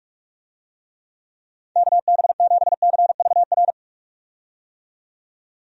Morse code, heard as a keyed tone, about 46 wpm, 700 Hz. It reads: KB7CVR
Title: USA Call Signs - 6 Characters - 45wpm